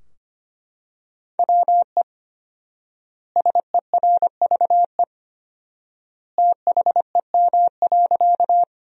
Morse code, heard as a keyed tone, about 25 wpm, 700 Hz. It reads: WE SERVE THEM.
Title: 4x Word-Spacing - Sentences from Top 400 Words - 25wpm